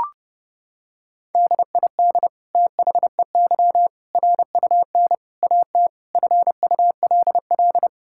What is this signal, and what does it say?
Morse code, heard as a keyed tone, about 32 wpm, 700 Hz. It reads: DID THEY RUN AT FULL
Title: Sentences from Top 1000 Words - Review Part 5 - 30wpm